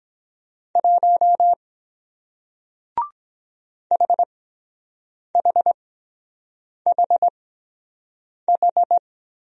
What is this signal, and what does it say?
Morse code, heard as a keyed tone, about 23 wpm, 700 Hz. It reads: 1 H H H H